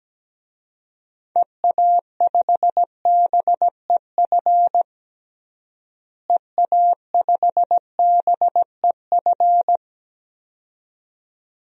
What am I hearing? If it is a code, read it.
Morse code, heard as a keyed tone, about 17 wpm, 700 Hz. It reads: EA5BEF EA5BEF
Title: DXCC-100 Call Signs - 6 Characters - Repeated 2x - 17wpm